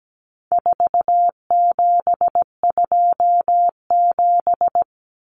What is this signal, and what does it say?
Morse code, heard as a keyed tone, about 17 wpm, 700 Hz. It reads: 4727
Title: Numbers - 4-Digit 17wpm